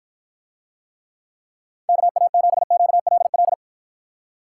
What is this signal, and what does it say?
Morse code, heard as a keyed tone, about 54 wpm, 700 Hz. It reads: KA7QLL